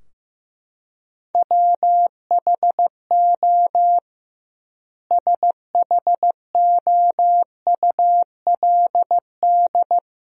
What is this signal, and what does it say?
Morse code, heard as a keyed tone, about 15 wpm, 700 Hz. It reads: WHO SHOULD
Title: 2x Word-Spacing - Sentences from Top 300 Words - 15wpm